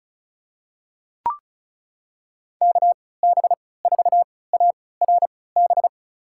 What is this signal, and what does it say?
Morse code, heard as a keyed tone, about 36 wpm, spaced as for 20 wpm, 700 Hz. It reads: KB4ARB
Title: USA Call Signs - 6 Characters - 3x Character Spacing - 35wpm